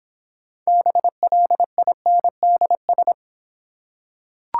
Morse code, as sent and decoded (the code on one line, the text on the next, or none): -... .-.. .. -. -.. ...
BLINDS